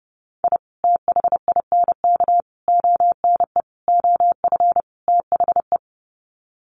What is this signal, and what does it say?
Morse code, heard as a keyed tone, about 30 wpm, 700 Hz. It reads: I THINK ONE OF THE
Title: Sets of 5 Words 30wpm